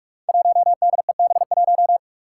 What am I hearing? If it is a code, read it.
Morse code, heard as a keyed tone, about 43 wpm, 700 Hz. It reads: 1DEB1
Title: Five Challenging Letter-Number Combinations 45wpm